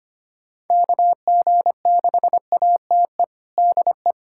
Morse code, heard as a keyed tone, about 25 wpm, 700 Hz. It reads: KG6ATE DE
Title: USA Callsign DE Callsign 25wpm